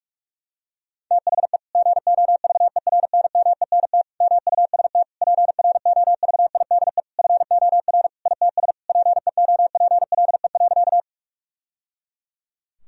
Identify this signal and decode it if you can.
Morse code, heard as a keyed tone, about 46 wpm, 700 Hz. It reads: THE GOVERNMENT MUST PROVIDE FOR ITS PEOPLE.
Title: Sentences from Top 1000 Words - Review Part 3 - 45wpm